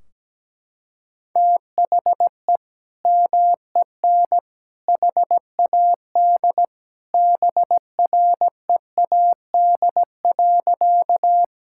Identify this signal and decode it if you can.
Morse code, heard as a keyed tone, about 17 wpm, 700 Hz. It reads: THE MEN HAD BREAD.
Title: Sentences from Top 1600 Words 17wpm